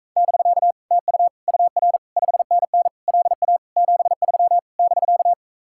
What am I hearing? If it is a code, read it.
Morse code, heard as a keyed tone, about 44 wpm, 700 Hz. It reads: <BK> TU UR 5NN LA 73 <BK>